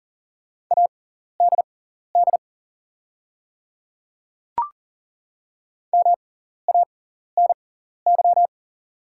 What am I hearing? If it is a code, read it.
Morse code, heard as a keyed tone, about 39 wpm, spaced as for 14 wpm, 700 Hz. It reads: ADD MANY
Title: 6x Character Spacing - Top 300 Words - 40wpm